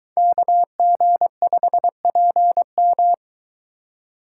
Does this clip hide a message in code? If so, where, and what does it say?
Yes: Morse code from 0.2 to 3.1 s, heard as a keyed tone, about 23 wpm, 700 Hz. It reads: KG5PM